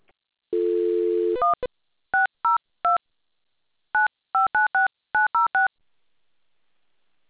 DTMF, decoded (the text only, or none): 1 602 9596906